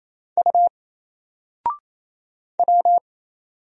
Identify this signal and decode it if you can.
Morse code, heard as a keyed tone, about 27 wpm, 700 Hz. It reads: U W
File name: Single-Letter-Number-28wpm